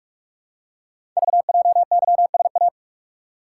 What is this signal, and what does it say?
Morse code, heard as a keyed tone, about 44 wpm, 700 Hz. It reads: VJYSA
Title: Code Groups - 5 Letters - 45wpm